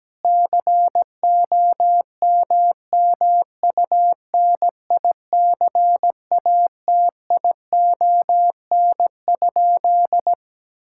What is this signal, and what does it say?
Morse code, heard as a keyed tone, about 17 wpm, 700 Hz. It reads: COMMUNICATION?
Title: Sentences from Top 1000 Words - Review Part 6 - 17wpm